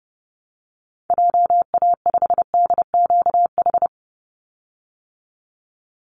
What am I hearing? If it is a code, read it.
Morse code, heard as a keyed tone, about 30 wpm, 700 Hz. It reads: JA5DQH
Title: Call Signs Repeated 3x - 30wpm